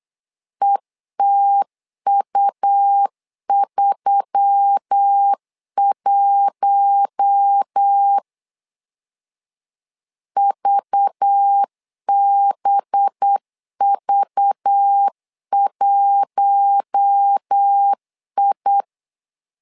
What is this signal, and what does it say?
Morse code, heard as a keyed tone, about 8 wpm, 800 Hz. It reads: ETU31 VBV1I
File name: LEZIONI / 5 LBVC / 5 LBV 8wpm 2837